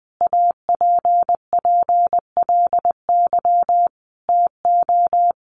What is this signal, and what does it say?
Morse code, heard as a keyed tone, about 20 wpm, 700 Hz. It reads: APPLY TO
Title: Sentences from Top 1200 Words 20wpm